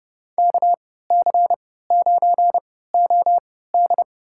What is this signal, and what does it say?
Morse code, heard as a keyed tone, about 30 wpm, spaced as for 17 wpm, 700 Hz. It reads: KC9OD
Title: USA Call Signs - 5 Characters - 3x Character Spacing - 30wpm